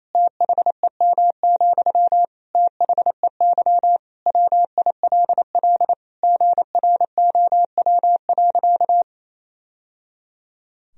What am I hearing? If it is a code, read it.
Morse code, heard as a keyed tone, about 28 wpm, 700 Hz. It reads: THEM, THEY WILL GROW.
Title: Sentences from Top 1000 Words - Review Part 1 - 28wpm